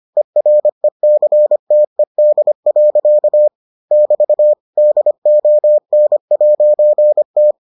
Morse code, heard as a keyed tone, about 25 wpm, 600 Hz. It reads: ERECTED. <BT> DON'T